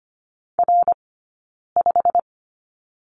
Morse code, heard as a keyed tone, about 25 wpm, spaced as for 9 wpm, 690 Hz. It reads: R5
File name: Lesson 15d - Number 5 - Two Characters 25wpm